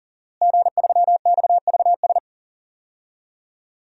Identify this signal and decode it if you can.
Morse code, heard as a keyed tone, about 40 wpm, 700 Hz. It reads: G3XVS
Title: DXCC-100 Call Signs - 5 Characters - Repeated 2x - 40wpm